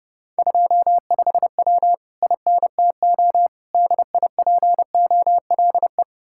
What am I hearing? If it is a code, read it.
Morse code, heard as a keyed tone, about 38 wpm, 700 Hz. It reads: 25W INTO DIPOLE